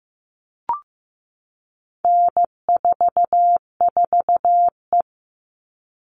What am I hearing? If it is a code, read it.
Morse code, heard as a keyed tone, about 15 wpm, 700 Hz. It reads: N44E
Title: All Cut Numbers - 4-Digit 15wpm